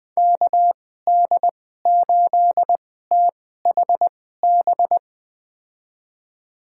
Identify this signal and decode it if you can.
Morse code, heard as a keyed tone, about 21 wpm, spaced as for 15 wpm, 700 Hz. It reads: KD8THB
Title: USA Call Signs - 6 Characters - 2x Character Spacing - 20wpm